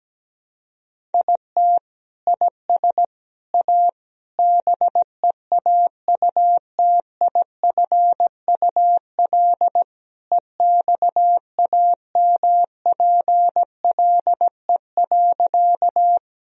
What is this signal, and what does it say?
Morse code, heard as a keyed tone, about 17 wpm, 700 Hz. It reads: IT IS A BEAUTIFUL EXAMPLE.
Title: Sentences from Top 1000 Words - Review Part 1 - 17wpm